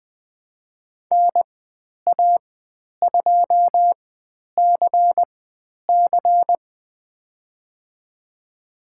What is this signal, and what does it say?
Morse code, heard as a keyed tone, about 20 wpm, spaced as for 10 wpm, 700 Hz. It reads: NA2CC
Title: 109 Lesson 18g - Letter m - Call Signs 20wpm